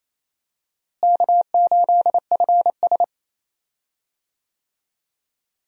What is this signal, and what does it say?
Morse code, heard as a keyed tone, about 28 wpm, 700 Hz. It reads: K8FS
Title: USA Call Signs - 4 Characters - Repeated 3x - 28wpm